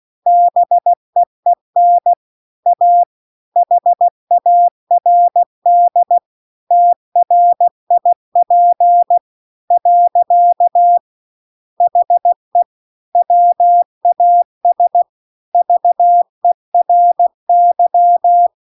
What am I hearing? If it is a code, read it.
Morse code, heard as a keyed tone, about 16 wpm, 700 Hz. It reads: BEEN A HARD TRIP . HE WAS VERY